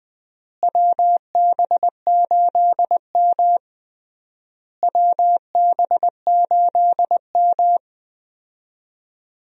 Morse code, heard as a keyed tone, about 20 wpm, 700 Hz. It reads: WB8M WB8M
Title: USA Call Signs - 4 Characters - Repeated 3x - 20wpm